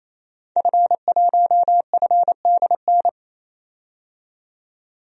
Morse code, heard as a keyed tone, about 28 wpm, 700 Hz. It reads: F1FDN
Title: DXCC-100 Call Signs - 5 Characters - 28wpm